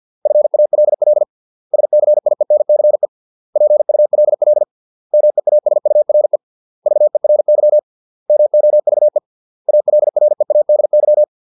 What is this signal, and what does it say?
Morse code, heard as a keyed tone, about 50 wpm, 600 Hz. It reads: FALL SCIENCE PULL MEASURE VERY MOVE ALREADY
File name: 632 words / 632 Words 50WPM